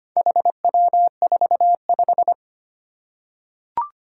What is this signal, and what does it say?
Morse code, heard as a keyed tone, about 25 wpm, 700 Hz. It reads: HW45
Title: Four Challenging Letter-Number Combinations 25wpm